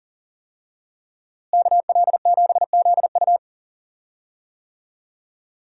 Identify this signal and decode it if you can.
Morse code, heard as a keyed tone, about 40 wpm, 700 Hz. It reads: KL7ZU
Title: USA Call Signs - 5 Characters - Repeated 2x - 40wpm